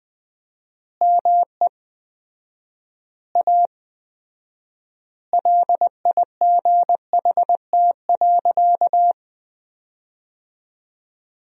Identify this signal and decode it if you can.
Morse code, heard as a keyed tone, about 20 wpm, 700 Hz. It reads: ME A LIGHT.
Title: 4x Word-Spacing - Sentences from Top 200 Words - 20wpm